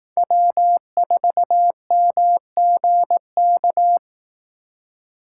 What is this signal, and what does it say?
Morse code, heard as a keyed tone, about 18 wpm, 700 Hz. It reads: W4MGK
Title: Speed Racing - USA 1x3 Call Signs 15wpm